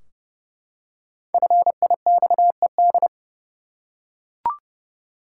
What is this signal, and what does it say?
Morse code, heard as a keyed tone, about 30 wpm, 700 Hz. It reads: FIXED
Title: Five Letter Words - 30wpm